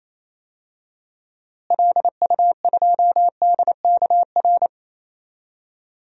Morse code, heard as a keyed tone, about 28 wpm, 700 Hz. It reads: LU2DKR
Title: DXCC-100 Call Signs - 6 Characters - Repeated 2x - 28wpm